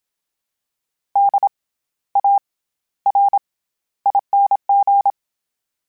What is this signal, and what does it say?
Morse code, heard as a keyed tone, about 27 wpm, 800 Hz. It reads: D A R ING
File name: Special INGs-mp3s / Special INGs-mp3s / SlowToFaster ing @25